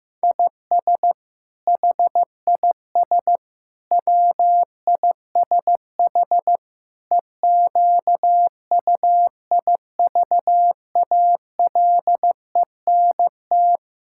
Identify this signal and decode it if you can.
Morse code, heard as a keyed tone, about 15 wpm, 700 Hz. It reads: IS HIS WISH EQUIVALENT
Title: Sentences from Top 1600 Words 15wpm